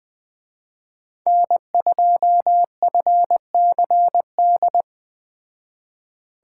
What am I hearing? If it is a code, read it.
Morse code, heard as a keyed tone, about 20 wpm, 700 Hz. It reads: N2FCD
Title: Speed Racing - USA 1x3 Call Signs 17wpm